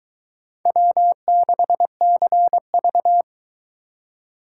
Morse code, heard as a keyed tone, about 22 wpm, 700 Hz. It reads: W6CV